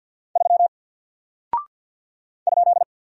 Morse code, heard as a keyed tone, about 49 wpm, 700 Hz. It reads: 3 ?
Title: Single Character (Letter:Number:Punctuation:Prosign) Alt 50wpm